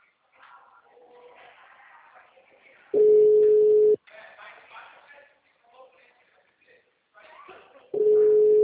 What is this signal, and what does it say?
Ringback tone (Europe, 425 Hz), with speech.